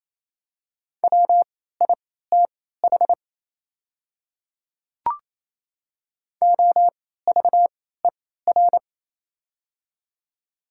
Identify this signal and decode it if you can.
Morse code, heard as a keyed tone, about 27 wpm, spaced as for 16 wpm, 700 Hz. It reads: WITH OVER
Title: 3x Character Spacing - Top 100 Words - 28wpm